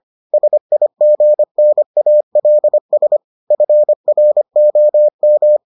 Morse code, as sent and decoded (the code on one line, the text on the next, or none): ... .. --. -. .- .-.. ... / ..-. .-. --- --
SIGNALS FROM